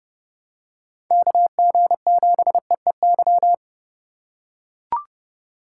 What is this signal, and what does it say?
Morse code, heard as a keyed tone, about 30 wpm, 700 Hz. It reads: KG7EEY
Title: USA Call Signs - 8x Character Spacing - 30wpm